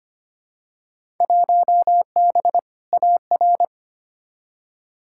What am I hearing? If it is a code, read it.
Morse code, heard as a keyed tone, about 25 wpm, 700 Hz. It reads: 1B AR
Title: Field Day Exchanges Encore - 25wpm